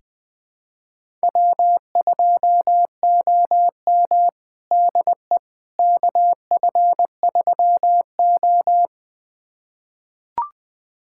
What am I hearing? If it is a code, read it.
Morse code, heard as a keyed tone, about 20 wpm, 700 Hz. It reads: W2OM DE KF3O
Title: USA Callsign DE Callsign - 4 Characters - 20wpm